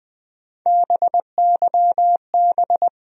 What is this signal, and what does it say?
Morse code, heard as a keyed tone, about 20 wpm, 700 Hz. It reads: BYB
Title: USA Call Sign Postfixes - 3 Letters 20wpm